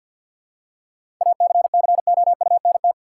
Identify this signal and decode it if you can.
Morse code, heard as a keyed tone, about 52 wpm, 700 Hz. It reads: ACCOUNT
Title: Seven Letter Words - 50wpm